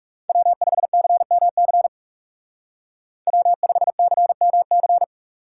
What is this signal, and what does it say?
Morse code, heard as a keyed tone, about 37 wpm, 700 Hz. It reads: W5CMC W5CMC
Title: Speed Racing - USA Call Signs 30wpm